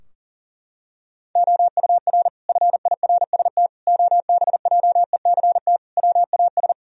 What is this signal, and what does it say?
Morse code, heard as a keyed tone, about 40 wpm, 700 Hz. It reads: OUR FIRST OBJECT WAS
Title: Sentences from Top 1000 Words - Review Part 3 - 40wpm